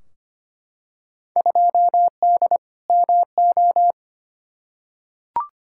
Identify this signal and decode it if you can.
Morse code, heard as a keyed tone, about 25 wpm, 700 Hz. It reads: 2D MO
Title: Field Day Exchanges - 25wpm